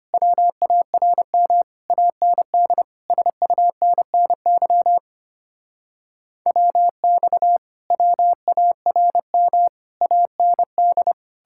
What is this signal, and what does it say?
Morse code, heard as a keyed tone, about 33 wpm, 700 Hz. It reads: WARM AND SUNNY WX WARM AND